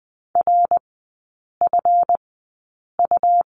Morse code, heard as a keyed tone, about 20 wpm, spaced as for 8 wpm, 700 Hz. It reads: RFU
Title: 152 Lesson 25e - Letter p - Three Characters 20wpm